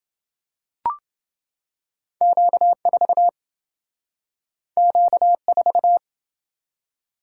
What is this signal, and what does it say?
Morse code, heard as a keyed tone, about 27 wpm, 700 Hz. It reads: Q4 Q4